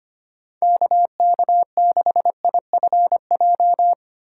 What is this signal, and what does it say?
Morse code, heard as a keyed tone, about 25 wpm, 700 Hz. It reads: KK6IFJ
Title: USA Call Signs - 6 Characters - 3x Character Spacing - 25wpm